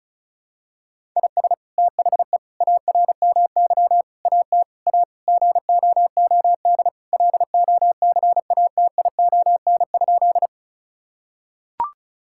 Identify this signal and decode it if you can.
Morse code, heard as a keyed tone, about 34 wpm, 700 Hz. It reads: IS THE ARMY AT A GOOD LOCATION?
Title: Sentences from Top 1000 Words - Review Part 6 - 35wpm